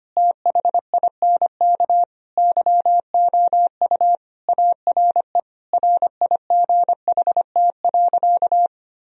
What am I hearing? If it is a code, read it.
Morse code, heard as a keyed tone, about 25 wpm, 700 Hz. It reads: THINK YOU ARE RIGHT.
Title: Sentences from Top 1000 Words - Review Part 4 - 25wpm